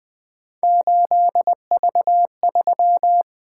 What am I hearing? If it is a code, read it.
Morse code, heard as a keyed tone, about 20 wpm, 700 Hz. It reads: 8V3
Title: Three Letter-Number Combinations - 20wpm